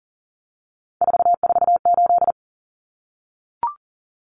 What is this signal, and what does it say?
Morse code, heard as a keyed tone, about 40 wpm, 700 Hz. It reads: 448
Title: Common Cut Numbers - 3-Digit 40wpm